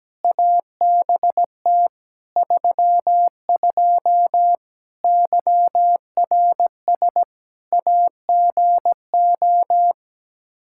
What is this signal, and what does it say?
Morse code, heard as a keyed tone, about 17 wpm, 700 Hz. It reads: ABT 32 YRS AGO
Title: Practice QSO - Exchange 3 - More Info - 17wpm